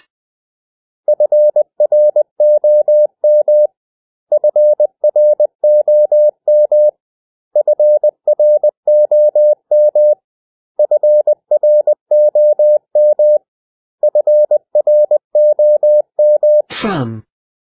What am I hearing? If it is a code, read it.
Morse code, heard as a keyed tone, about 20 wpm, 600 Hz. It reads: FROM FROM FROM FROM FROM
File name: cw / 100 Common English Words